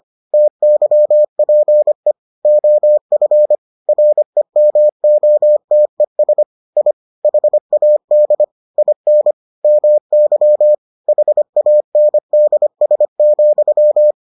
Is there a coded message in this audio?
Yes: Morse code from 0.3 to 14.1 s, heard as a keyed tone, about 25 wpm, 600 Hz. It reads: TYPE OF REMOTES I HAD IN MY HANDS,